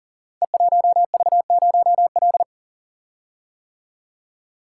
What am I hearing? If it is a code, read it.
Morse code, heard as a keyed tone, about 43 wpm, 700 Hz. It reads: E1V0L
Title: Code Groups - 5 Letter:Numbers - 40wpm